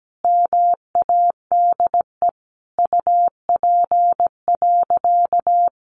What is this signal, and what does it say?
Morse code, heard as a keyed tone, about 17 wpm, 700 Hz. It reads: MADE UP.
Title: Sentences from Top 500 Words 17wpm